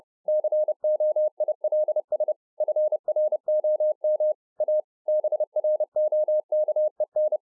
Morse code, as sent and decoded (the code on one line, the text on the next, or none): -.-. --- .. .-.. ... / ..-. .-. --- -- / .- / -... .-. --- -.- . -.
COILS FROM A BROKEN